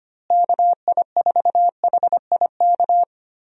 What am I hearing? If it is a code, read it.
Morse code, heard as a keyed tone, about 25 wpm, 700 Hz. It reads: KI4HIK
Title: USA Call Signs - 6 Characters - Repeated 3x - 25wpm